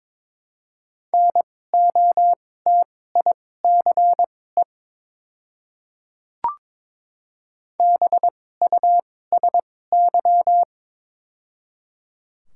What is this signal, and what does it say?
Morse code, heard as a keyed tone, about 22 wpm, spaced as for 16 wpm, 690 Hz. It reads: NOTICE BUSY
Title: 2x Character Spacing - Top 500 Words - 22wpm